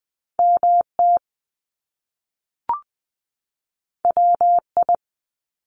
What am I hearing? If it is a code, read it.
Morse code, heard as a keyed tone, about 20 wpm, 700 Hz. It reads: MT WI
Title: US States 20wpm